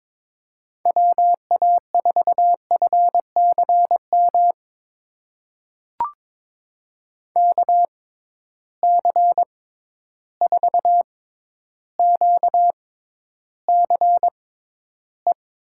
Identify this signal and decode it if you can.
Morse code, heard as a keyed tone, about 22 wpm, spaced as for 11 wpm, 700 Hz. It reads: WA4FCM KC4QCE
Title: USA Call Signs - 6 Characters - 6x Character Spacing - 22wpm